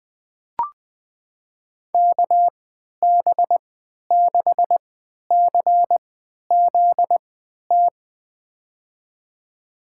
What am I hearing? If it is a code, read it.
Morse code, heard as a keyed tone, about 20 wpm, spaced as for 11 wpm, 700 Hz. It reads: KB6CZT
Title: USA Call Signs - 6 Characters - 3x Character Spacing - 20wpm